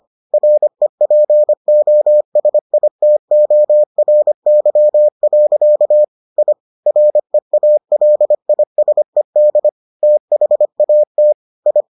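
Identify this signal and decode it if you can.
Morse code, heard as a keyed tone, about 25 wpm, 600 Hz. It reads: REPOSITORY. I REALISED THAT I
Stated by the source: Morse edition of the Foundations of Amateur Radio episodes.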